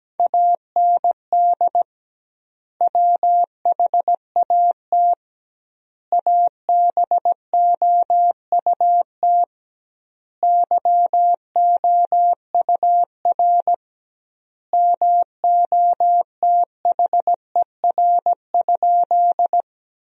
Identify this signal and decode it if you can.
Morse code, heard as a keyed tone, about 17 wpm, 700 Hz. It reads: AND WHAT ABOUT YOUR MOTHER?
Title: 2x Word-Spacing - Sentences from Top 200 Words - 17wpm